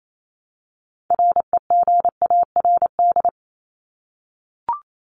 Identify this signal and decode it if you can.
Morse code, heard as a keyed tone, about 28 wpm, 700 Hz. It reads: REGARD
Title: ING Suffix - Top 100 Words with Base Words 28wpm